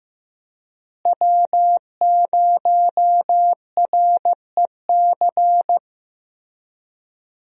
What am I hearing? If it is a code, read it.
Morse code, heard as a keyed tone, about 16 wpm, 700 Hz. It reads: W0REC
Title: USA Call Signs - 5 Characters - Repeated 3x - 15wpm